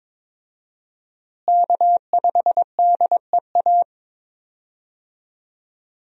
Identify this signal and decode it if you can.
Morse code, heard as a keyed tone, about 22 wpm, 700 Hz. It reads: K5DEA
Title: USA Call Signs - 5 Characters - Repeated 2x - 22wpm